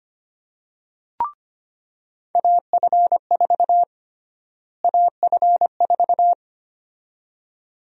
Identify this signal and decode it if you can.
Morse code, heard as a keyed tone, about 25 wpm, 700 Hz. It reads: AF4 AF4
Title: USA Call Sign Prefixes - Repeated 2x - 25wpm